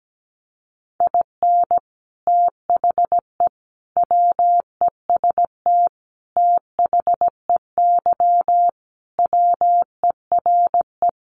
Morse code, heard as a keyed tone, about 17 wpm, 700 Hz. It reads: IN THE WEST THEY WERE
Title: Sentences from Top 1300 Words 17wpm